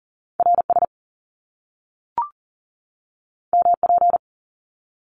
Morse code, heard as a keyed tone, about 40 wpm, 700 Hz. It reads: RS MP
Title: Top Two Letter Combinations in Words 40wpm